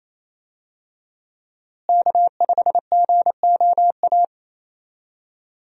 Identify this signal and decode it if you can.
Morse code, heard as a keyed tone, about 27 wpm, 700 Hz. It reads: K5GOA
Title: USA Call Signs - 5 Characters - Repeated 3x - 28wpm